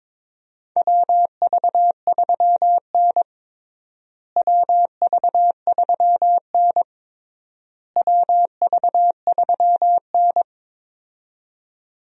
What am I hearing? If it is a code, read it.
Morse code, heard as a keyed tone, about 21 wpm, 700 Hz. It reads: WV3N WV3N WV3N
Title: USA Call Signs - 4 Characters - Repeated 3x - 22wpm